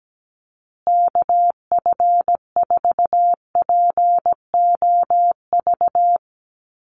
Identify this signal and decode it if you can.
Morse code, heard as a keyed tone, about 17 wpm, 700 Hz. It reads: KF4POV
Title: USA Call Signs Repeated 3x 17wpm